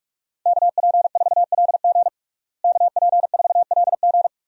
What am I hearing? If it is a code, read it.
Morse code, heard as a keyed tone, about 41 wpm, 700 Hz. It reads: KP4LG KP4LG
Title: DXCC-100 Call Signs - 5 Characters - Repeated 2x - 45wpm